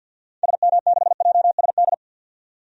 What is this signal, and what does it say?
Morse code, heard as a keyed tone, about 46 wpm, 700 Hz. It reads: SM6JSD
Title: DXCC-100 Call Signs - 6 Characters - 50wpm